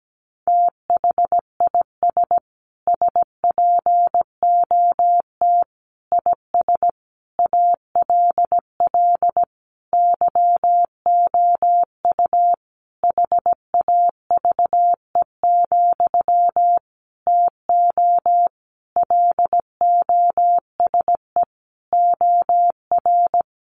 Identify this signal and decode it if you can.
Morse code, heard as a keyed tone, about 17 wpm, 700 Hz. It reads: THIS SPOT IS ALL YOU HAVE, TO LOSE OR
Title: Sentences from Top 1100 Words 17wpm